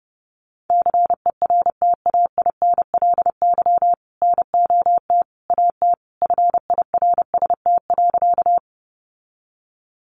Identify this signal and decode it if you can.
Morse code, heard as a keyed tone, about 30 wpm, 700 Hz. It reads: CERTAINLY NOT AT FIRST.
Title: Sentences from Top 900 Words 30wpm